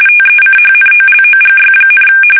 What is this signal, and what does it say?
RTTY, about 45.45 baud. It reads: TECHNOLOGY,
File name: fsk850